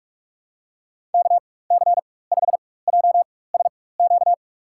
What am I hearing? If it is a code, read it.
Morse code, heard as a keyed tone, about 45 wpm, spaced as for 21 wpm, 700 Hz. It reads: KC5JSQ
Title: USA Call Signs - 6 Characters - 4x Character Spacing - 45wpm